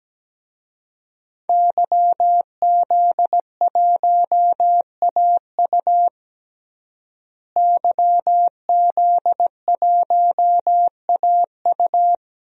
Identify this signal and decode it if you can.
Morse code, heard as a keyed tone, about 17 wpm, 700 Hz. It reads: YZ1AU YZ1AU
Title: DXCC-100 Call Signs - 5 Characters - Repeated 2x - 17wpm